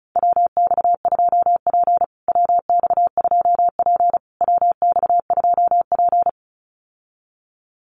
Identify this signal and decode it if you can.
Morse code, heard as a keyed tone, about 34 wpm, 700 Hz. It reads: WX2P WX2P WX2P
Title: USA Call Signs Repeated 3x 35wpm